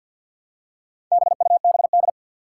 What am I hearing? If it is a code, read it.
Morse code, heard as a keyed tone, about 51 wpm, 700 Hz. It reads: BUBD
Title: Four Challenging Letter-Number Combinations 50wpm